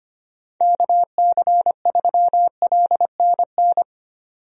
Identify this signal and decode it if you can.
Morse code, heard as a keyed tone, about 25 wpm, 700 Hz. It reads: KC3LNN
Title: USA Call Signs - 6 Characters - 25wpm